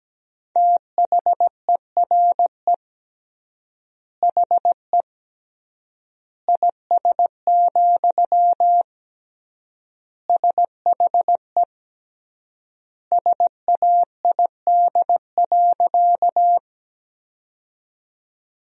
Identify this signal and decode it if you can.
Morse code, heard as a keyed tone, about 17 wpm, 700 Hz. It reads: THERE HE IS, SHE SAID.
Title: 3x Word-Spacing - Sentences from Top 100 Words - 17wpm